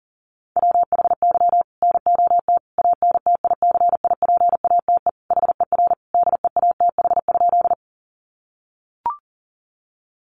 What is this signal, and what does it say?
Morse code, heard as a keyed tone, about 37 wpm, 700 Hz. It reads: WHY NOT ANTICIPATE HER DEATH?
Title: Sentences from Top 1500 Words 40wpm